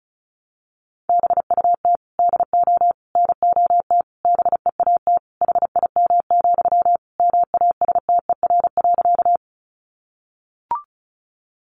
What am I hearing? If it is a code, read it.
Morse code, heard as a keyed tone, about 34 wpm, 700 Hz. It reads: BUT DO NOT BEAT HIM, MASTER.
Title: Sentences from Top 1300 Words 35wpm